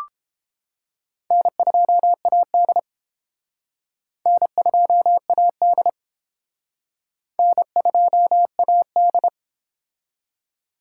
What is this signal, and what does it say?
Morse code, heard as a keyed tone, about 30 wpm, 700 Hz. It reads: N2AD N2AD N2AD